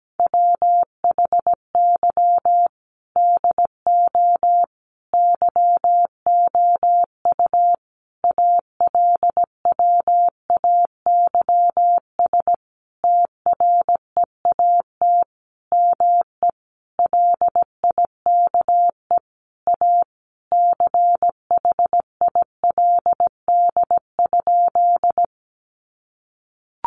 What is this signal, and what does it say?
Morse code, heard as a keyed tone, about 17 wpm, 700 Hz. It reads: WHY DO YOU ALWAYS TREAT ME LIKE A CHILD?
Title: Sentences from Top 1300 Words 17wpm